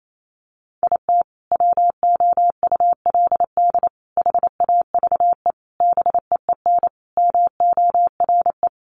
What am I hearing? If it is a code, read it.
Morse code, heard as a keyed tone, about 28 wpm, 700 Hz. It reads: IT WOULD HAVE BEEN MORE
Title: Sentences from Top 400 Words 28wpm